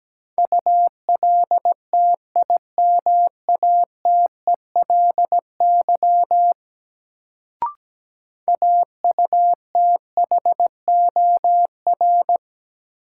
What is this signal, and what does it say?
Morse code, heard as a keyed tone, about 17 wpm, 700 Hz. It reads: ULTIMATELY AUTHOR
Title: Sentences from Top 1200 Words - New Word Drill - 17wpm